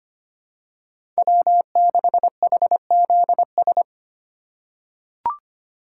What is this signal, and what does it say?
Morse code, heard as a keyed tone, about 25 wpm, 700 Hz. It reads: W6HZS
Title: USA Call Signs - 5 Characters - 3x Character Spacing - 25wpm